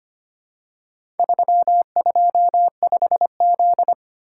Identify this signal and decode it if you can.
Morse code, heard as a keyed tone, about 25 wpm, 700 Hz. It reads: 325Z
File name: Four Letter-Number Combinations - 25wpm